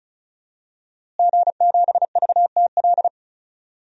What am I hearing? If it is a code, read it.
Morse code, heard as a keyed tone, about 36 wpm, 700 Hz. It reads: G7VTL
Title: DXCC-100 Call Signs - 5 Characters - Repeated 3x - 35wpm